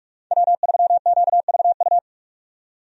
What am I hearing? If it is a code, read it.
Morse code, heard as a keyed tone, about 46 wpm, 700 Hz. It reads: W3QVU
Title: USA Call Signs - 5 Characters - 45wpm